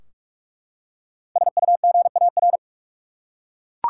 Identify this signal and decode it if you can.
Morse code, heard as a keyed tone, about 46 wpm, 700 Hz. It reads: SUGAR
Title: Five Letter Words - 45wpm